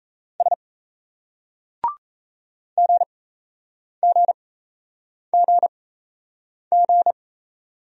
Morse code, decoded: S GGGG

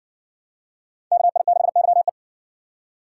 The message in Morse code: KI6YE

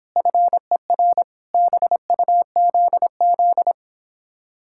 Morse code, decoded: FER BUZZ

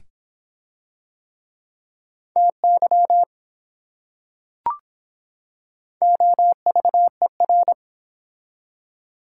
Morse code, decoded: TY OVER

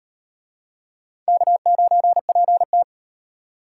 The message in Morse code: K9PT

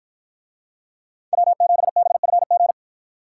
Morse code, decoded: W7BFG

